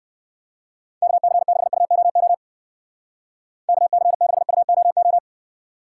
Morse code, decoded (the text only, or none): KC6UOO KC6UOO